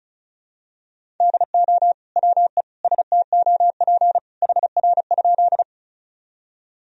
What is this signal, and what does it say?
Morse code, heard as a keyed tone, about 34 wpm, 700 Hz. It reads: DO WE STOP HR?
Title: Sentences from Top 300 Words and Abbreviations 35wpm